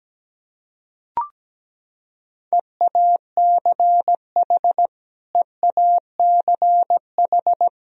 Morse code, heard as a keyed tone, about 17 wpm, 700 Hz. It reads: EACH EACH
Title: Top 100 Words Repeated 17wpm